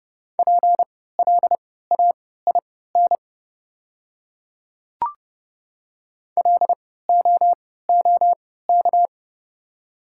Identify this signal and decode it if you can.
Morse code, heard as a keyed tone, about 30 wpm, spaced as for 17 wpm, 700 Hz. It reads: PLAIN LOOK
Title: 3x Character Spacing - Top 500 Words - 30wpm